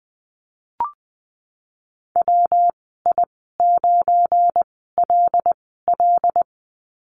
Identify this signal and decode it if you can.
Morse code, heard as a keyed tone, about 20 wpm, spaced as for 15 wpm, 700 Hz. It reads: WI9LL
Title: 207 Lesson 33g - Letter j - Call Signs 20wpm